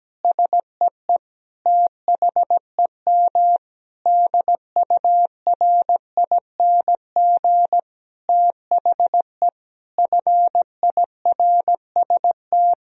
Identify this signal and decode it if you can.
Morse code, heard as a keyed tone, about 17 wpm, 700 Hz. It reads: SEE THEM DURING THE FIRST
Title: Sentences from Top 1000 Words - Review Part 2 - 17wpm